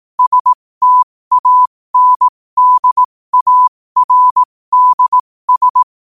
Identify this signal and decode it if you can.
Morse code, heard as a keyed tone, about 18 wpm, spaced as for 15 wpm, 1000 Hz. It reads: STANDARDS
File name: qotdfiles / 15wpm